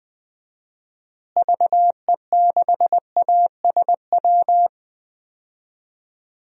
Morse code, heard as a keyed tone, about 20 wpm, 700 Hz. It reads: VE6ASW
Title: DXCC-100 Call Signs - 6 Characters - Repeated 2x - 20wpm